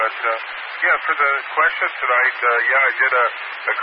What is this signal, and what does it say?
Emergency Alert System (SAME) tones, heard as harsh buzzing bursts, with speech. It is heard from 0.5 to 3.1 s.